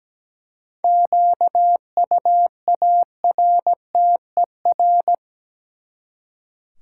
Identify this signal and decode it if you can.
Morse code, heard as a keyed tone, about 17 wpm, 700 Hz. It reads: QUARTER
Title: Seven Letter Words - 17wpm